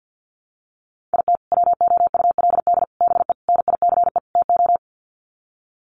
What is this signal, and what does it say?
Morse code, heard as a keyed tone, about 50 wpm, 700 Hz. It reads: IT WOULD BE NICE TO
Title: Sets of 5 Words Encore2 50wpm